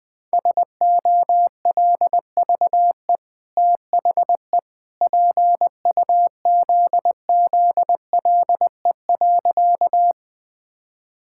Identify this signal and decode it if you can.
Morse code, heard as a keyed tone, about 20 wpm, 700 Hz. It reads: SOLVE THE PUZZLE.
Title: Easy Sentences with Hints - 20wpm